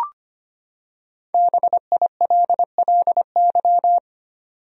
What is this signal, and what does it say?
Morse code, heard as a keyed tone, about 25 wpm, 700 Hz. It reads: BILLY
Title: USA Common Names - Max 5 Letters - 25wpm